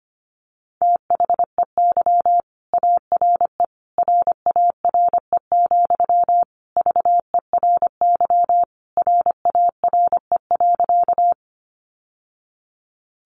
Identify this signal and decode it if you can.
Morse code, heard as a keyed tone, about 25 wpm, 700 Hz. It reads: THEY ARE RARE, VERY RARE.
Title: Sentences from Top 1100 Words 25wpm